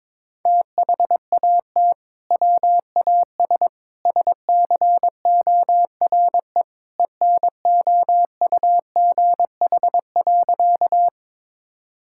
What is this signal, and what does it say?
Morse code, heard as a keyed tone, about 22 wpm, 700 Hz. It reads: THAT WAS SCORE ENOUGH.
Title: Sentences from Top 1600 Words 22wpm